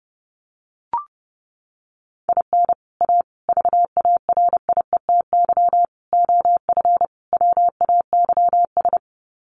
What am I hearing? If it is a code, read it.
Morse code, heard as a keyed tone, about 30 wpm, 700 Hz. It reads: IN A VARIETY OF WAYS
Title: Sets of 5 Words Encore 30wpm